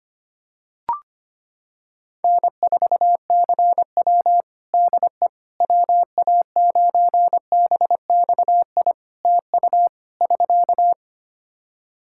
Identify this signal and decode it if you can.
Morse code, heard as a keyed tone, about 25 wpm, 700 Hz. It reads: N4CW DE WA9BXI TU <SK>
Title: Practice QSO - Exchange 4 - Ending - 25wpm